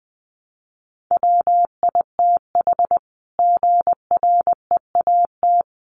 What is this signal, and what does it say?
Morse code, heard as a keyed tone, about 20 wpm, 700 Hz. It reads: WITH GREAT
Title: Sentences from Top 900 Words 20wpm